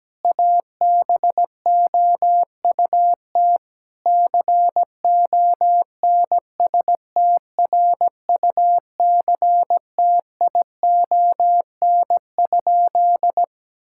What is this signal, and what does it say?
Morse code, heard as a keyed tone, about 17 wpm, 700 Hz. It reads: ABOUT CONSTRUCTION?
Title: Sentences from Top 1600 Words 17wpm